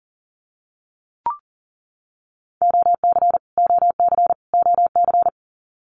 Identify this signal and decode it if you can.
Morse code, heard as a keyed tone, about 40 wpm, 700 Hz. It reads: OC OC OC
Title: Abbreviations Repeated 3x - 40wpm